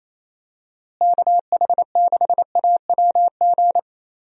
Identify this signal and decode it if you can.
Morse code, heard as a keyed tone, about 28 wpm, 700 Hz. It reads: KH6AWG
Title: DXCC-100 Call Signs - 6 Characters - Repeated 2x - 28wpm